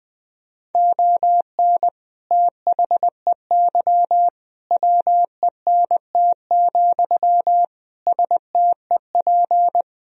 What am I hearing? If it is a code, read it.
Morse code, heard as a keyed tone, about 20 wpm, 700 Hz. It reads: ON THEY WENT, STEP
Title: Sentences from Top 1000 Words - Review Part 2 - 20wpm